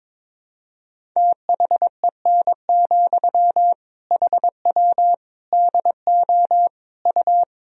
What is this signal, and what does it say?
Morse code, heard as a keyed tone, about 22 wpm, 700 Hz. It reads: THEN, HW DO U
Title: Sentences from Top 100 Words and Abbreviations 22wpm